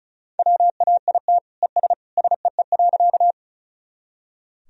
Morse code, heard as a keyed tone, about 36 wpm, 700 Hz. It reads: WAIT ES SEE.
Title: Sentences from Top 500 Words and Abbreviations 35wpm